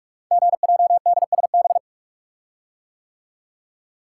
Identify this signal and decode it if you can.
Morse code, heard as a keyed tone, about 44 wpm, 690 Hz. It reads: GJDSB
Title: Code Groups - 5 Letters - 45wpm